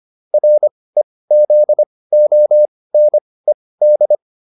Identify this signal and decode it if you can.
Morse code, heard as a keyed tone, about 25 wpm, spaced as for 18 wpm, 600 Hz. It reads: REZONED